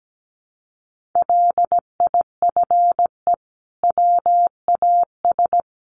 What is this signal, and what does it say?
Morse code, heard as a keyed tone, about 17 wpm, 700 Hz. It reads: LIFE WAS
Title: Sentences from Top 1500 Words 17wpm